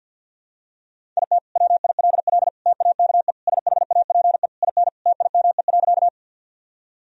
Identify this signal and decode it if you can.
Morse code, heard as a keyed tone, about 52 wpm, 700 Hz. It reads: IT WILL TAKE SHAPE IN TIME.
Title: Sentences from Top 1000 Words - Review Part 3 - 50wpm